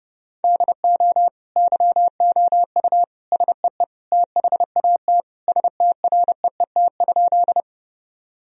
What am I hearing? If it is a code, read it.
Morse code, heard as a keyed tone, about 29 wpm, 700 Hz. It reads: DO YOU SEE THAT STREET?
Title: Sentences from Top 1000 Words - Review Part 3 - 30wpm